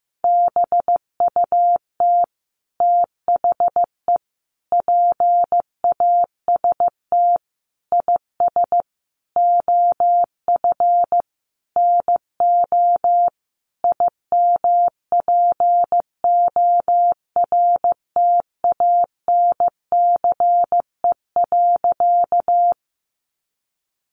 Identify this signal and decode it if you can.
Morse code, heard as a keyed tone, about 15 wpm, 700 Hz. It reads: BUT THE PAST IS OF NO IMPORTANCE.
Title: Sentences from Top 1200 Words 15wpm